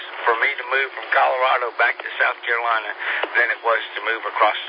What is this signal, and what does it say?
Emergency Alert System (SAME) tones, heard as harsh buzzing bursts, with speech. It is heard from 0.4 to 4.2 s.